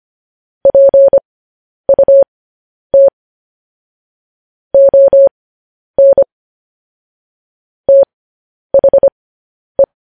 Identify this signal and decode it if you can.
Morse code, heard as a keyed tone, about 25 wpm, spaced as for 10 wpm, 550 Hz. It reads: PUT ON THE
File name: podcast / News headlines-1x-25x10